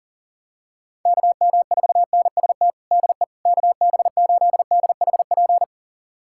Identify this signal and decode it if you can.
Morse code, heard as a keyed tone, about 40 wpm, 700 Hz. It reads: KM4NST DE KB8DHP
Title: USA Callsign DE Callsign 40wpm